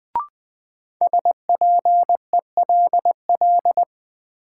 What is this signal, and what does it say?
Morse code, heard as a keyed tone, about 20 wpm, 700 Hz. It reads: SPELL
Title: Sentences from Top 1000 Words - Review Part 1 - New Word Drill - 20wpm